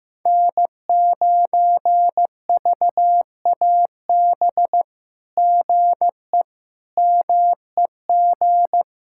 Morse code, heard as a keyed tone, about 15 wpm, 700 Hz. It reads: N9VAB GE MEG
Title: SOTA QSO Practice 15wpm